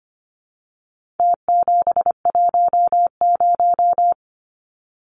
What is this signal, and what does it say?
Morse code, heard as a keyed tone, about 25 wpm, 700 Hz. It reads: T710
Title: Numbers - Contest Style - 4-Digit 25wpm